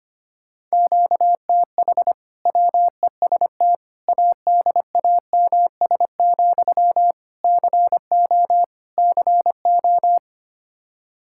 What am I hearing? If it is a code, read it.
Morse code, heard as a keyed tone, about 25 wpm, 700 Hz. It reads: QTH WEST ADAMS, CO CO
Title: Practice QSO - Exchange 1 - Beginning - 25wpm